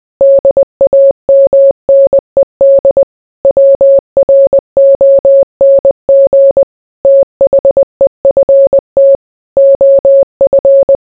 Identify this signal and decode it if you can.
Morse code, heard as a keyed tone, about 20 wpm, 550 Hz. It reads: DAMNED WRONG THEFT OF